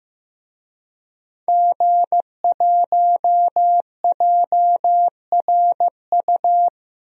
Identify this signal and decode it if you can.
Morse code, heard as a keyed tone, about 15 wpm, 700 Hz. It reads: G1JRU
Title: DXCC-100 Call Signs - 5 Characters - 15wpm